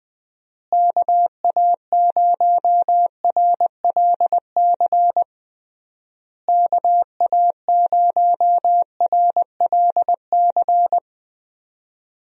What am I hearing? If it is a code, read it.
Morse code, heard as a keyed tone, about 21 wpm, 700 Hz. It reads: KA0RLC KA0RLC
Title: USA Call Signs - 6 Characters - Repeated 3x - 20wpm